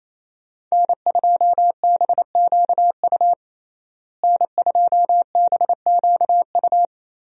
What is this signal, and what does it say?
Morse code, heard as a keyed tone, about 28 wpm, 700 Hz. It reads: N2BQU N2BQU
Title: USA Call Signs - 5 Characters - Repeated 3x - 28wpm